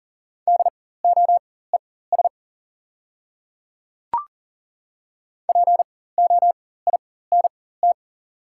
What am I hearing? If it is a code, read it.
Morse code, heard as a keyed tone, about 40 wpm, spaced as for 19 wpm, 690 Hz. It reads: DOES POINT
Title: 4x Character Spacing - Top 200 Words - 40wpm